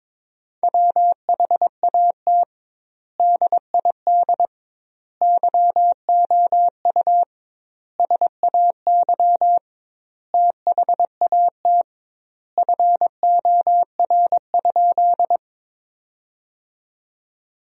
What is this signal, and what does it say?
Morse code, heard as a keyed tone, about 22 wpm, 700 Hz. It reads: WHAT DID YOU SAY THAT FOR?
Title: 2x Word-Spacing - Sentences from Top 200 Words - 22wpm